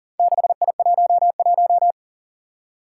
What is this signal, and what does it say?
Morse code, heard as a keyed tone, about 38 wpm, 700 Hz. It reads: 6I11
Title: Four Challenging Letter-Number Combinations 40wpm